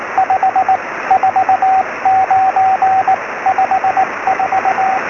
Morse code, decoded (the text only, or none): 5 4 9 5 4